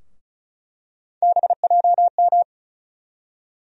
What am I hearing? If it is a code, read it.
Morse code, heard as a keyed tone, about 34 wpm, 700 Hz. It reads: BJM
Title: USA Call Sign Postfixes - 3 Letters 35wpm